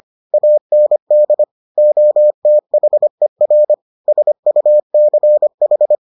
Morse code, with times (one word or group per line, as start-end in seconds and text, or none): 0.3-1.4 AND
1.8-3.7 OTHER
4.1-6.0 SUCH